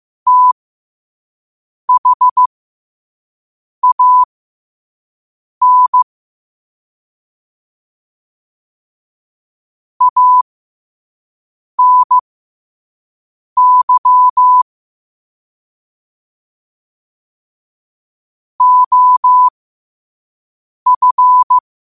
Morse code, heard as a keyed tone, about 15 wpm, spaced as for 5 wpm, 1000 Hz. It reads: THAN ANY OF
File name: qotdfiles / 5wpm